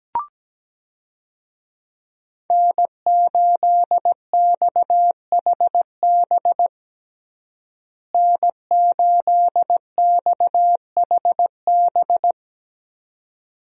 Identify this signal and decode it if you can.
Morse code, heard as a keyed tone, about 17 wpm, 700 Hz. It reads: N8XHB N8XHB
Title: USA Call Signs - 5 Characters - Repeated 2x - 17wpm